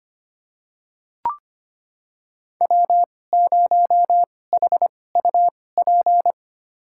Morse code, heard as a keyed tone, about 25 wpm, spaced as for 18 wpm, 700 Hz. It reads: W0HUP
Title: USA Call Signs - 5 Characters - 2x Character Spacing - 25wpm